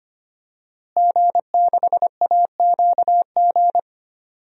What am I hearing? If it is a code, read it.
Morse code, heard as a keyed tone, about 25 wpm, 700 Hz. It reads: G6AQG
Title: DXCC-100 Call Signs - 5 Characters - Repeated 3x - 25wpm